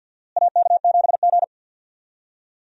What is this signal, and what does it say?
Morse code, heard as a keyed tone, about 52 wpm, 700 Hz. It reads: AK7G
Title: USA Call Signs - 4 Characters - Repeated 2x - 50wpm